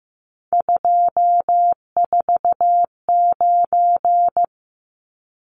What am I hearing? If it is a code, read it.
Morse code, heard as a keyed tone, about 15 wpm, 700 Hz. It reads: 249
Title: Numbers - 3-Digit 15wpm